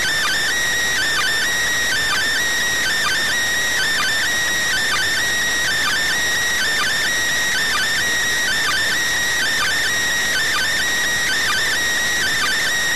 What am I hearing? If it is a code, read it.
Two-tone teleprinter signal, about 80 Hz shift.